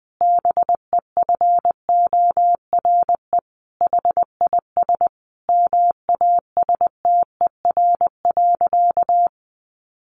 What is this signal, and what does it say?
Morse code, heard as a keyed tone, about 20 wpm, 700 Hz. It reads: BEFORE HIS MASTER.
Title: Sentences from Top 1300 Words 20wpm